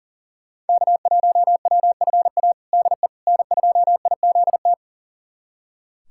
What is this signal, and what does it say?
Morse code, heard as a keyed tone, about 40 wpm, 700 Hz. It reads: K1WFA DE N2IZT
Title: USA Callsign DE Callsign - 5 Characters - 40wpm